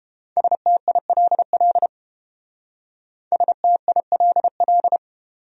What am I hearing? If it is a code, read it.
Morse code, heard as a keyed tone, about 29 wpm, 700 Hz. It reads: STILL STILL